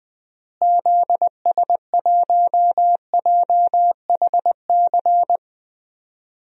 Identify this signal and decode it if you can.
Morse code, heard as a keyed tone, about 20 wpm, 700 Hz. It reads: ZS1JHC
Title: DXCC-100 Call Signs - 6 Characters - Repeated 3x - 20wpm